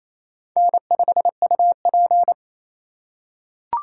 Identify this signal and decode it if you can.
Morse code, heard as a keyed tone, about 28 wpm, 700 Hz. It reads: N5UP
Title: USA Call Signs - 4 Characters - 4x Character Spacing - 28wpm